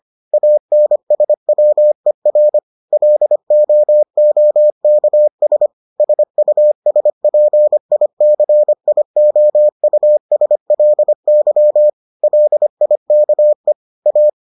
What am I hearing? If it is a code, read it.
Morse code, heard as a keyed tone, about 25 wpm, 600 Hz. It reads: ANSWER LOOKS SUSPICIOUSLY LIKE A